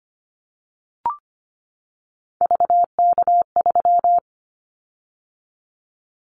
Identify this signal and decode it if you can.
Morse code, heard as a keyed tone, about 25 wpm, 700 Hz. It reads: VK3
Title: Callsign Prefixes for Top 50 DXCC - 25wpm